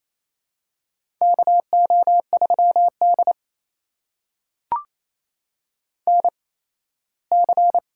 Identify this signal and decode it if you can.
Morse code, heard as a keyed tone, about 28 wpm, spaced as for 17 wpm, 700 Hz. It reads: KO3D NC
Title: USA Call Signs - 4 Characters - 8x Character Spacing - 28wpm